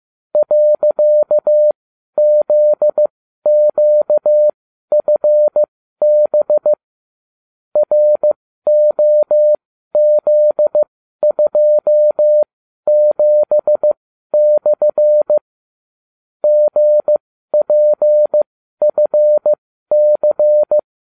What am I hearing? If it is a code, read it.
Morse code, heard as a keyed tone, about 15 wpm, spaced as for 12 wpm, 600 Hz. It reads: .ZQFB ROZ27/ GPFC